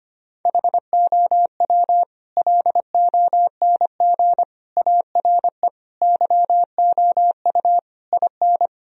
Morse code, heard as a keyed tone, about 25 wpm, 700 Hz. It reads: HOW LONG ARE YOU IN
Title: Sentences from Top 1000 Words - Review Part 2 - 25wpm